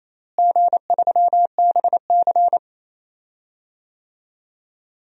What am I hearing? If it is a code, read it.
Morse code, heard as a keyed tone, about 28 wpm, 700 Hz. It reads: G3BC
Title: DXCC-100 Call Signs - 4 Characters - Repeated 3x - 28wpm